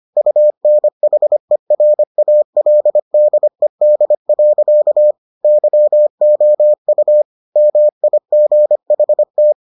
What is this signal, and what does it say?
Morse code, heard as a keyed tone, about 25 wpm, 600 Hz. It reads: UNHERALDED. YOU MIGHT